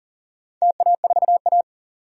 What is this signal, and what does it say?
Morse code, heard as a keyed tone, about 40 wpm, 700 Hz. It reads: TA4A
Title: DXCC-100 Call Signs - 4 Characters - 40wpm